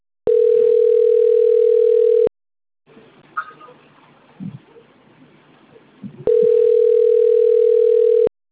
Ringback tone (North America), 440 + 480 Hz.
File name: rg-1600-unknown- - - .17712